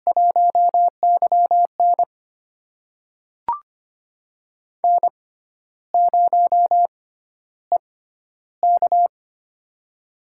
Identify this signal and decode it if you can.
Morse code, heard as a keyed tone, about 25 wpm, spaced as for 12 wpm, 700 Hz. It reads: 1YN N0EK
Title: USA Call Signs - 6x Character Spacing - 25wpm